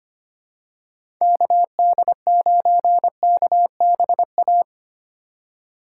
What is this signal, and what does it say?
Morse code, heard as a keyed tone, about 25 wpm, 700 Hz. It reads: KD9KBA
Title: USA Call Signs - 6 Characters - 2x Character Spacing - 25wpm